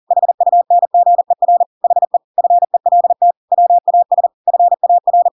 Morse code, heard as a keyed tone, about 40 wpm, 700 Hz. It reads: HUNGER HE FELT WAS FAR